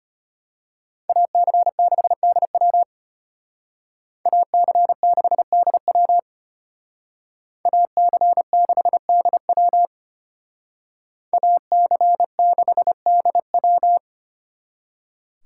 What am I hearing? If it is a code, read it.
Morse code, heard as a keyed tone, about 33 wpm, 700 Hz. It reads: AC6DW AC6DW AC6DW AC6DW